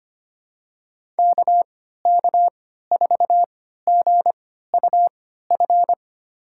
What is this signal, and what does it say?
Morse code, heard as a keyed tone, about 25 wpm, spaced as for 14 wpm, 700 Hz. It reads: KK4GUF
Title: USA Call Signs - 6 Characters - 3x Character Spacing - 25wpm